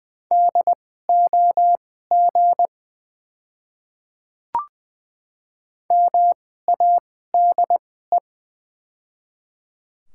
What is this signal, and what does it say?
Morse code, heard as a keyed tone, about 20 wpm, spaced as for 14 wpm, 690 Hz. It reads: DOG MADE
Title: 2x Character Spacing - Top 500 Words - 20wpm